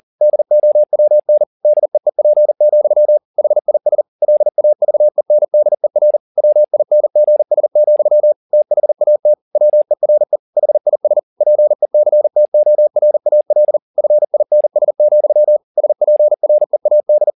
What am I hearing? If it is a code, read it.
Morse code, heard as a keyed tone, about 40 wpm, 600 Hz. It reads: DOWN DEEP, HIS LAVENDER WINGS, THAT WERE HIS PECTORAL FINS, SPREAD